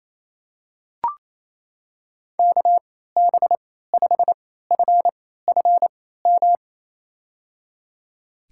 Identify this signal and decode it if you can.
Morse code, heard as a keyed tone, about 28 wpm, spaced as for 16 wpm, 700 Hz. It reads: KB5FFM
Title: USA Call Signs - 6 Characters - 3x Character Spacing - 28wpm